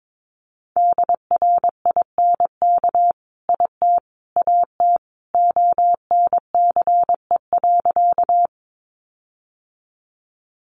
Morse code, heard as a keyed tone, about 22 wpm, 700 Hz. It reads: DRINK IT AT ONCE.
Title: Sentences from Top 1500 Words 22wpm